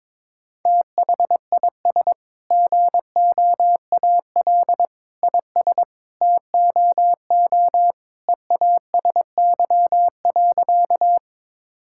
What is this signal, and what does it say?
Morse code, heard as a keyed tone, about 22 wpm, 700 Hz. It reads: THIS GOAL IS TOO EASY.
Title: Sentences from Top 1000 Words - Review Part 5 - 22wpm